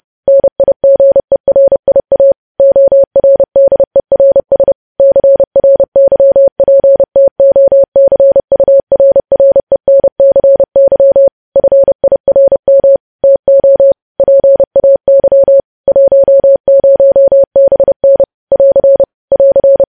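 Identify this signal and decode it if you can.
Morse code, heard as a keyed tone, about 30 wpm, 550 Hz. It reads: NIGERIA ORDERS CRYPTOCURRENCY FIRM TO PAY 10BN <AR> <AR>
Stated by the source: Morse code news headlines sent at 30 words per minute.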